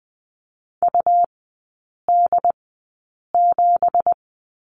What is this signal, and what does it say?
Morse code, heard as a keyed tone, about 20 wpm, spaced as for 8 wpm, 700 Hz. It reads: UD7
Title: 191 Lesson 31e - Letter v - Three Characters 20wpm